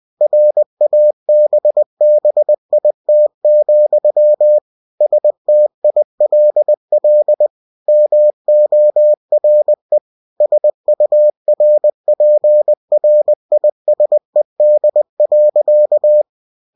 Morse code, heard as a keyed tone, about 20 wpm, 600 Hz. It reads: RABBIT, STILL MORE SURPRISED.
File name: cw-books / winnie-the-pooh / winnie-20wpm-page-0015